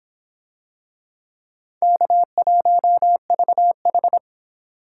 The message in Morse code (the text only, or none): K1VH